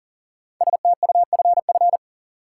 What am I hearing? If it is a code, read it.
Morse code, heard as a keyed tone, about 37 wpm, 700 Hz. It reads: STUFF
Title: Sentences from Top 1400 Words - New Word Drill - 40wpm